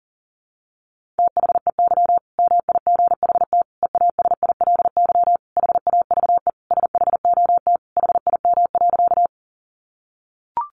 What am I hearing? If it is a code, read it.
Morse code, heard as a keyed tone, about 40 wpm, 700 Hz. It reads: THEY MIGHT EASILY HAVE SHOT HIM.
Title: Sentences from Top 1100 Words 40wpm